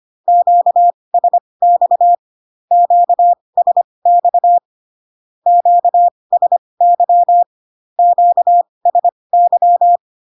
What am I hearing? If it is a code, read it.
Morse code, heard as a keyed tone, about 25 wpm, spaced as for 20 wpm, 700 Hz. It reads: QSX QSX QSY QSY
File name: cw / Q-codes most common repeated 3 times 15 25wpm ik1hge